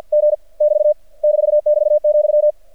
Morse code, heard as a keyed tone, about 49 wpm, 600 Hz. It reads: G <BT> -<BT>,
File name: archived / CRSnet